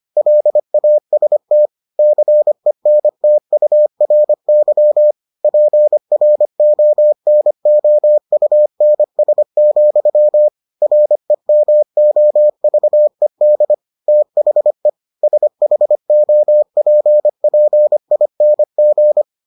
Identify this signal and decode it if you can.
Morse code, heard as a keyed tone, about 25 wpm, 600 Hz. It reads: LAST CENTURY PRONOUNS, REMOVED THE SHOPPING